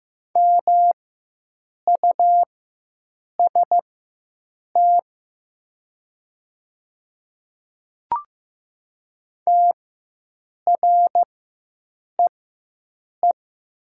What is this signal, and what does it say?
Morse code, heard as a keyed tone, about 15 wpm, spaced as for 7 wpm, 700 Hz. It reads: MUST TREE
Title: 4x Character Spacing - Top 300 Words - 15wpm